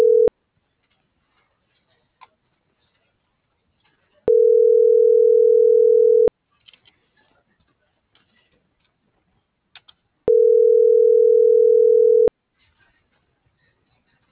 Ringback tone (North America), 440 + 480 Hz.